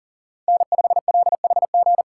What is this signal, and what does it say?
Morse code, heard as a keyed tone, about 40 wpm, 700 Hz. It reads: N5LHG
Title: USA Call Signs - 3x Character Spacing - 40wpm